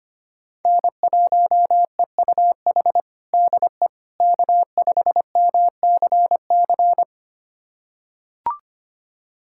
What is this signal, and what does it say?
Morse code, heard as a keyed tone, about 25 wpm, 700 Hz. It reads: N1EUH DE K5MCC
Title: USA Callsign DE Callsign - 5 Characters - 25wpm